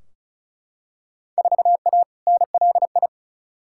Morse code, heard as a keyed tone, about 34 wpm, 700 Hz. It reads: 4A NLI
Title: Field Day Exchanges Encore - 35wpm